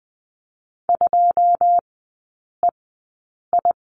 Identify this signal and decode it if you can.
Morse code, heard as a keyed tone, about 20 wpm, spaced as for 9 wpm, 690 Hz. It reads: 2EI
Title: 081 Lesson 14e - Number 2 - Three Characters 20wpm